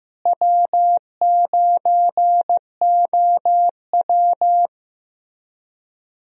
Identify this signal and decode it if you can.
Morse code, heard as a keyed tone, about 15 wpm, 700 Hz. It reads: W9OW
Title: USA Call Signs - 4 Characters - Repeated 2x - 15wpm